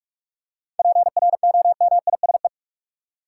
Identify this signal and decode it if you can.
Morse code, heard as a keyed tone, about 47 wpm, 700 Hz. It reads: PROMISE
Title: Seven Letter Words - 45wpm